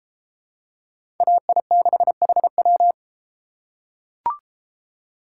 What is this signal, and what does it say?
Morse code, heard as a keyed tone, about 31 wpm, 700 Hz. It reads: AI6HW